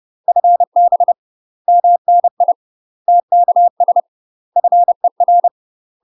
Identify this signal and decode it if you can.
Morse code, heard as a keyed tone, about 30 wpm, 700 Hz. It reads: FB MNI TKS FER
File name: UK MASTER QSOs TRAINING / 30WPM / UK MASTER QSOs 0041